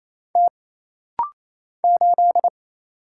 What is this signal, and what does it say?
Morse code, heard as a keyed tone, about 28 wpm, 700 Hz. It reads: T 8
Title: Single Letter-Number Alt - Rapid-Fire 28wpm